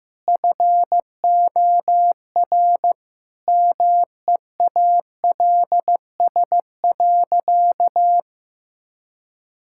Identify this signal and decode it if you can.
Morse code, heard as a keyed tone, about 15 wpm, 700 Hz. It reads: FOR MEALS.
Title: Pangram Challenge v2 - 15wpm